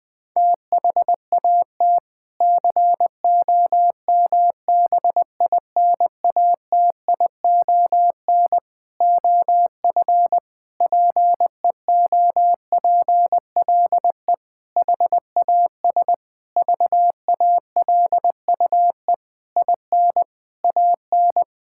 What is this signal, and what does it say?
Morse code, heard as a keyed tone, about 20 wpm, 700 Hz. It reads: THAT COMBINATION OF PEOPLE HAS VALUE IN AN
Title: Sentences from Top 1000 Words - Review Part 6 - 20wpm